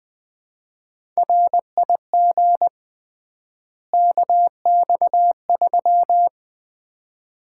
Morse code, heard as a keyed tone, about 20 wpm, 700 Hz. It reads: RIG KX3
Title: Common QSO Phrases - 3x Word-Spacing - 20wpm